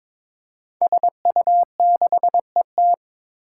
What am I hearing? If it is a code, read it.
Morse code, heard as a keyed tone, about 22 wpm, 700 Hz. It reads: SU6ET
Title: Five Challenging Letter-Number Combinations 22wpm